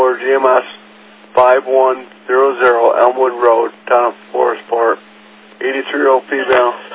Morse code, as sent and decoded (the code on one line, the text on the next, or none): none